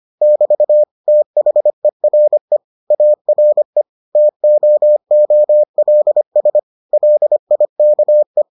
Morse code, heard as a keyed tone, about 25 wpm, 600 Hz. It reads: <BT> THERE ARE TOOLS LIKE